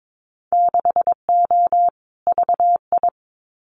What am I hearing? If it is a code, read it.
Morse code, heard as a keyed tone, about 22 wpm, 700 Hz. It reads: 6O VI
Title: Winter Field Day Exchanges Encore 22wpm